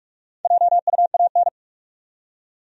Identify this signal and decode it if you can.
Morse code, heard as a keyed tone, about 44 wpm, 700 Hz. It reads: JUAN
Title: USA Common Names - Max 5 Letters - 45wpm